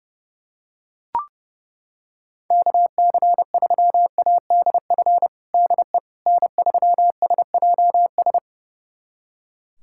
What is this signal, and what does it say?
Morse code, heard as a keyed tone, about 32 wpm, 700 Hz. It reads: KC3ADF DE N3SJS
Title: USA Callsign DE Callsign 30wpm